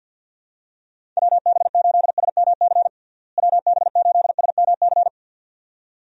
Morse code, heard as a keyed tone, about 50 wpm, 700 Hz. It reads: WB8SMC WB8SMC
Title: USA Call Signs - 6 Characters - Repeated 2x - 50wpm